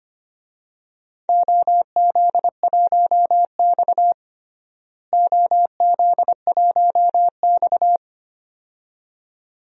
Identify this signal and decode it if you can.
Morse code, heard as a keyed tone, about 25 wpm, 700 Hz. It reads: OZ1X OZ1X
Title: DXCC-100 Call Signs - 4 Characters - Repeated 2x - 25wpm